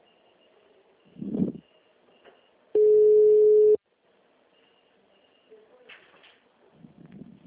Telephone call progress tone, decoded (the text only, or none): ringback tone